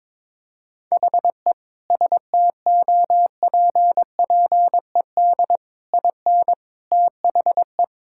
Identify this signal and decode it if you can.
Morse code, heard as a keyed tone, about 22 wpm, 700 Hz. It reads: HE STOPPED IN THE
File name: practice / 22wpm / Sentences with 3-Word Phrases - 22wpm